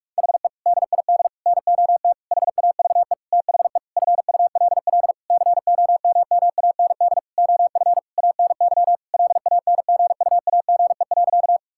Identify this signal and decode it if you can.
Morse code, heard as a keyed tone, about 41 wpm, 700 Hz. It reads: HE DID NOT HAVE THE FULL COMMAND OF ANY LANGUAGE.
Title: Sentences from Top 1600 Words 45wpm